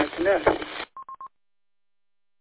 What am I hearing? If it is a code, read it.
Speech. There is no code.